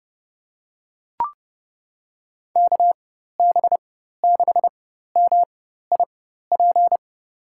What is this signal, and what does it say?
Morse code, heard as a keyed tone, about 30 wpm, spaced as for 14 wpm, 700 Hz. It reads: KB6MIP
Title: USA Call Signs - 4x Character Spacing - 30wpm